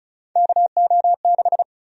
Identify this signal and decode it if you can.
Morse code, heard as a keyed tone, about 34 wpm, 700 Hz. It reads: KO6
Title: USA Call Sign Prefixes - Repeated 2x - 35wpm